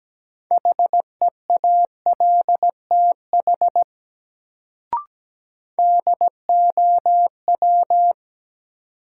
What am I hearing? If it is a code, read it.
Morse code, heard as a keyed tone, about 17 wpm, 700 Hz. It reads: HEALTH DOW
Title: Sets of 5 Words - New Word Drill - 17wpm